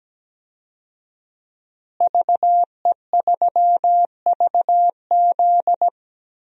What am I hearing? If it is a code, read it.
Morse code, heard as a keyed tone, about 17 wpm, 700 Hz. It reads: VE3VZ
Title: DXCC-100 Call Signs - 5 Characters - Repeated 2x - 17wpm